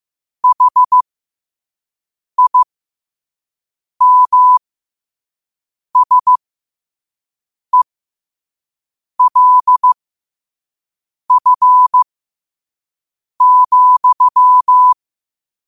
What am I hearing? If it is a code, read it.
Morse code, heard as a keyed tone, about 15 wpm, spaced as for 5 wpm, 1000 Hz. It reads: HIMSELF,
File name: qotdfiles / 5wpm